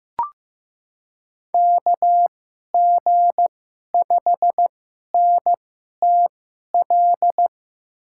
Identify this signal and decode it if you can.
Morse code, heard as a keyed tone, about 15 wpm, spaced as for 11 wpm, 700 Hz. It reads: KG5NTL
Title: USA Call Signs - 2x Character Spacing - 15wpm